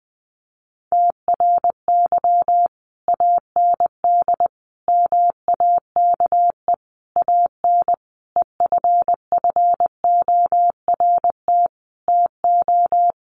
Morse code, heard as a keyed tone, about 20 wpm, 700 Hz. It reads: TRY AND MAKE AN EFFORT TO
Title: Sentences from Top 1200 Words 20wpm